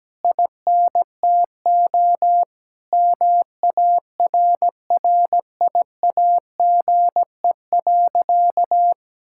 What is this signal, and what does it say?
Morse code, heard as a keyed tone, about 17 wpm, 700 Hz. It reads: INTO MARRIAGE.
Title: Sentences from Top 1000 Words - Review Part 6 - 17wpm